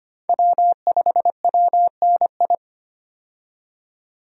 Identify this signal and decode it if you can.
Morse code, heard as a keyed tone, about 25 wpm, 700 Hz. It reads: W5WNI
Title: USA Call Signs - 5 Characters - 25wpm